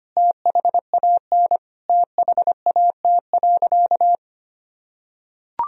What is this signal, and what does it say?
Morse code, heard as a keyed tone, about 25 wpm, 700 Hz. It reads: THAN THAT.
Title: Sentences from Top 1000 Words - Review Part 3 - 25wpm